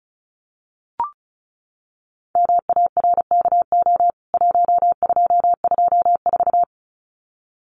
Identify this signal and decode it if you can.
Morse code, heard as a keyed tone, about 34 wpm, 700 Hz. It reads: MARKO 1224
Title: CWT Exchange 35wpm